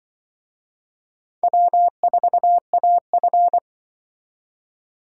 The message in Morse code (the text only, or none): W4AF